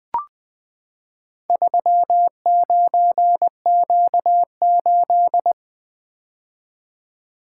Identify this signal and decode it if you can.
Morse code, heard as a keyed tone, about 20 wpm, 700 Hz. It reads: 39Q8
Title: Four Letter-Number Combinations - 20wpm